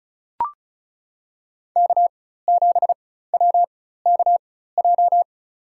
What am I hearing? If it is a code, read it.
Morse code, heard as a keyed tone, about 36 wpm, spaced as for 16 wpm, 700 Hz. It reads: K7WKJ
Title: USA Call Signs - 4x Character Spacing - 35wpm